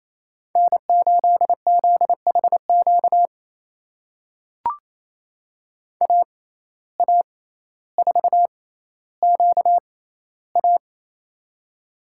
Morse code, heard as a keyed tone, about 27 wpm, spaced as for 14 wpm, 700 Hz. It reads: N8ZHQ AA4QA
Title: USA Call Signs - 5 Characters - 6x Character Spacing - 28wpm